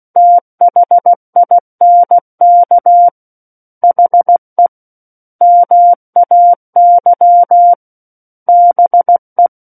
Morse code, heard as a keyed tone, about 16 wpm, 700 Hz. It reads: THINK HE MAY BE
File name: Book One / 07-Chapter 7 - 16 WPM